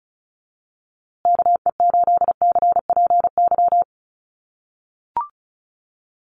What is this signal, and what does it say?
Morse code, heard as a keyed tone, about 34 wpm, 700 Hz. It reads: KE8CPY
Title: USA Call Signs Repeated 3x 35wpm